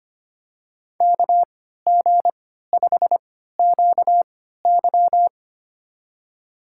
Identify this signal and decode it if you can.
Morse code, heard as a keyed tone, about 25 wpm, spaced as for 14 wpm, 700 Hz. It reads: KG5QY
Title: USA Call Signs - 5 Characters - 3x Character Spacing - 25wpm